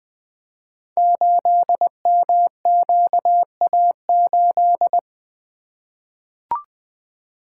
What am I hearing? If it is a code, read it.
Morse code, heard as a keyed tone, about 20 wpm, 700 Hz. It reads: 8MQA8
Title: Code Groups - 5 Letter:Numbers - 20wpm